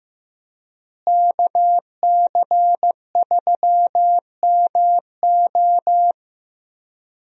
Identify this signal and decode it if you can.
Morse code, heard as a keyed tone, about 15 wpm, 700 Hz. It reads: KC3MO
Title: USA Call Signs - 5 Characters - Repeated 3x - 15wpm